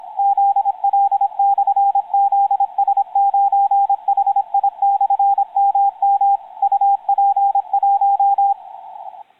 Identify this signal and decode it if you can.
Morse code, heard as a keyed tone, about 26 wpm, 800 Hz, with radio noise. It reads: ZL/ZS9HI/MM UP1
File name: zl-zs9hi-mm-on-15m-cw-heading-towards-heard-island-vk0ek